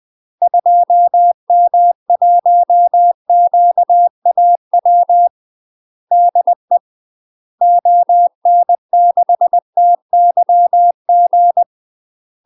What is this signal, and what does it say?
Morse code, heard as a keyed tone, about 20 wpm, 700 Hz. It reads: 2M1QAW DE ON6TYG
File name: UK MASTER QSOs TRAINING / 20WPM / UK MASTER QSOs 0010